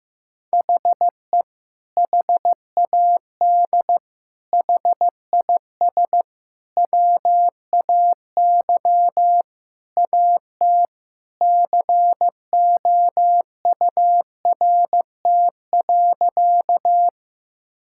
Morse code, heard as a keyed tone, about 16 wpm, 700 Hz. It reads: HE HAD HIS WAY AT COURT.
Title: Sentences from Top 1600 Words 15wpm